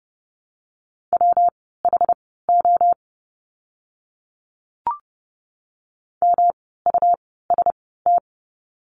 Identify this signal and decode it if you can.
Morse code, heard as a keyed tone, about 30 wpm, spaced as for 17 wpm, 700 Hz. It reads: WHO MUST
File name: course / 30:10wpm / 237 Lesson 38f - Letter q - Words 30wpm